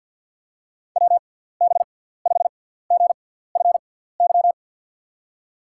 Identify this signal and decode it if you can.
Morse code, heard as a keyed tone, about 51 wpm, 700 Hz. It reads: W B 5 G F Y
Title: USA Call Signs - 6 Characters - 6x Character Spacing - 50wpm